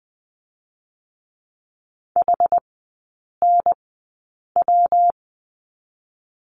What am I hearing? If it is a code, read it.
Morse code, heard as a keyed tone, about 20 wpm, 700 Hz. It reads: H N W
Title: 178 Lesson 29e - Slash - Three Characters 20wpm